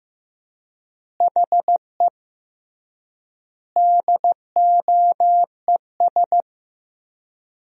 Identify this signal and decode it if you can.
Morse code, heard as a keyed tone, about 16 wpm, 700 Hz. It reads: HE DOES
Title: 3x Word-Spacing - Sentences from Top 200 Words - 15wpm